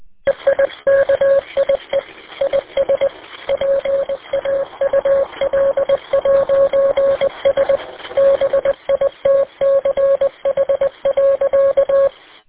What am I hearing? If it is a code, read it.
Morse code, heard as a keyed tone, about 20 wpm, 550 Hz. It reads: EIKIE IS PAUL'S BITCH.